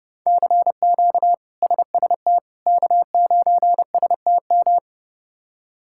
Morse code, heard as a keyed tone, about 30 wpm, 700 Hz. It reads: CQ SST K9STM
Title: K1USN SST QSO Practice - 30wpm